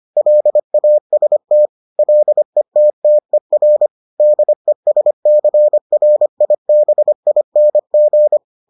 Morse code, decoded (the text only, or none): LAST LETTER DESCRIBING